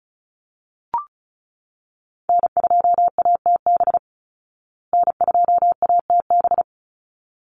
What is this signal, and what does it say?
Morse code, heard as a keyed tone, about 34 wpm, 700 Hz. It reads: N2ATB N2ATB
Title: Call Signs Repeated 3x - 35wpm